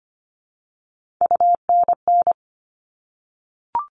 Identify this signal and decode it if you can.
Morse code, heard as a keyed tone, about 25 wpm, 700 Hz. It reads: UNN
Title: All Cut Numbers - 3-Digit 25wpm